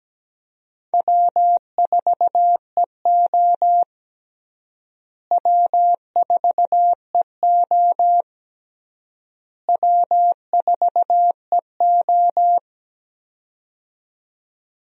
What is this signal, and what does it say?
Morse code, heard as a keyed tone, about 17 wpm, 700 Hz. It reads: W4EO W4EO W4EO
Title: USA Call Signs - 4 Characters - Repeated 3x - 17wpm